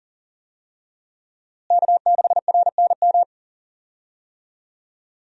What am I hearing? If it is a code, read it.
Morse code, heard as a keyed tone, about 40 wpm, 700 Hz. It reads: K6RNM
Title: USA Call Signs - 5 Characters - Repeated 3x - 40wpm